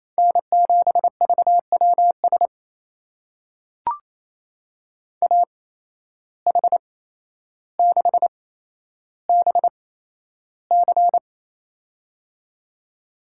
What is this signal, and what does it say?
Morse code, heard as a keyed tone, about 28 wpm, spaced as for 12 wpm, 700 Hz. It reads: N7VWS AH6BC
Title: USA Call Signs - 5 Characters - 8x Character Spacing - 28wpm